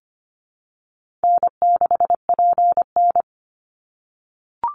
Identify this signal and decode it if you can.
Morse code, heard as a keyed tone, about 25 wpm, 700 Hz. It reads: N6PN
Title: Easy Call Signs - 25wpm